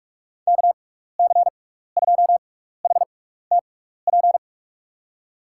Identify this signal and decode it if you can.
Morse code, heard as a keyed tone, about 44 wpm, spaced as for 15 wpm, 700 Hz. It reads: KC2HTP
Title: USA Call Signs - 6 Characters - 6x Character Spacing - 45wpm